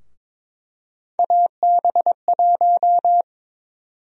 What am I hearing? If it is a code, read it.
Morse code, heard as a keyed tone, about 22 wpm, 700 Hz. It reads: AB1
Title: USA Call Sign Prefixes - 22wpm